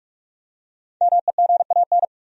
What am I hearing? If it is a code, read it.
Morse code, heard as a keyed tone, about 47 wpm, 700 Hz. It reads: MEGAN
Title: USA Common Names - 45wpm